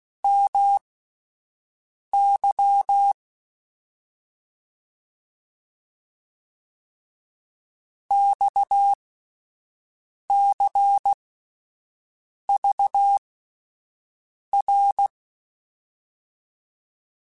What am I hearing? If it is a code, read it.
Morse code, heard as a keyed tone, about 16 wpm, spaced as for 5 wpm, 800 Hz. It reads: MY XCVR
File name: QSOs / QSOs-mp3s / WD0FBT 1@10